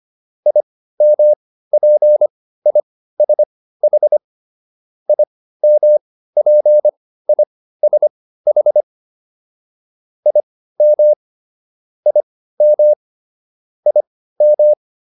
Morse code, heard as a keyed tone, about 25 wpm, spaced as for 15 wpm, 600 Hz. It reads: IMPISH IMPISH IM IM IM